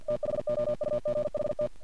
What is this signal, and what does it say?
Morse code, heard as a keyed tone, about 50 wpm, 600 Hz. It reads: THOUGHT
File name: archived / CRSnet